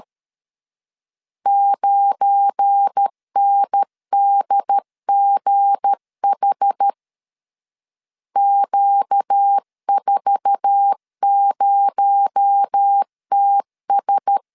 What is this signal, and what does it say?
Morse code, heard as a keyed tone, about 13 wpm, 800 Hz. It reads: 9NDGH Q40TS
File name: LEZIONI / 8 YJ45 / 8 YJ4 12wpm 2151